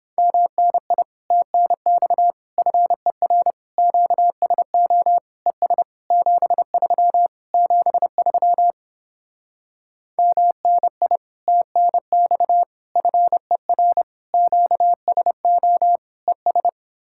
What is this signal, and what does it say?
Morse code, heard as a keyed tone, about 29 wpm, 700 Hz. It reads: MNI TNX FER QSO ES 73 73 MNI TNX FER QSO ES